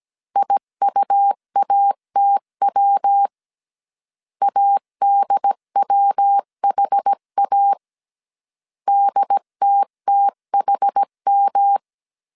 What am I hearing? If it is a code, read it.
Morse code, heard as a keyed tone, about 17 wpm, 800 Hz. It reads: IUATW ADWHA DTTHM
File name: CW / DET / 1 3 1500C S A 16wpm 2141